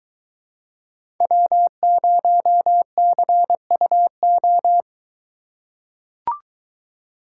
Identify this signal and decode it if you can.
Morse code, heard as a keyed tone, about 23 wpm, 700 Hz. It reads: W0CUO